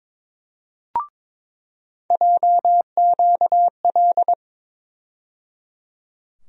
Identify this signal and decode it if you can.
Morse code, heard as a keyed tone, about 22 wpm, 700 Hz. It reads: JQL
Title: USA Call Sign Postfixes - 3 Letters 22wpm